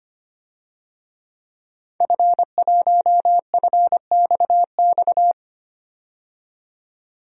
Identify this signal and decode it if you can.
Morse code, heard as a keyed tone, about 25 wpm, 700 Hz. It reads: F1FXX
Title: DXCC-100 Call Signs - 5 Characters - 25wpm